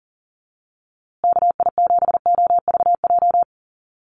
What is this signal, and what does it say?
Morse code, heard as a keyed tone, about 40 wpm, 700 Hz. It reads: KI7OVJ
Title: USA Call Signs Repeated 3x 40wpm